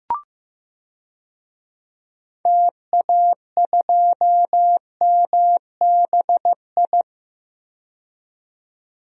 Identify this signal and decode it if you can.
Morse code, heard as a keyed tone, about 15 wpm, 700 Hz. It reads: TA2MBI
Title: DXCC-100 Call Signs - 6 Characters - 15wpm